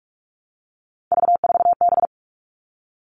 Morse code, heard as a keyed tone, about 44 wpm, 700 Hz. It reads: V4B
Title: All Cut Numbers - 3-Digit 45wpm